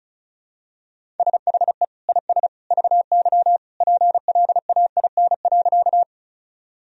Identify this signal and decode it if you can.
Morse code, heard as a keyed tone, about 36 wpm, 700 Hz. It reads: SHE IS VY PLAIN.
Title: Sentences from Top 400 Words and Abbreviations 35wpm